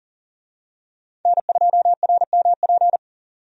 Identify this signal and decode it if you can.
Morse code, heard as a keyed tone, about 40 wpm, 700 Hz. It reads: N2RMP
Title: USA Call Signs - 5 Characters - 40wpm